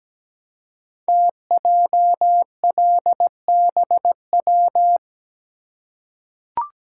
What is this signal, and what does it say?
Morse code, heard as a keyed tone, about 17 wpm, 700 Hz. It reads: TJLBW
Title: Code Groups - 5 Letter:Numbers - 17wpm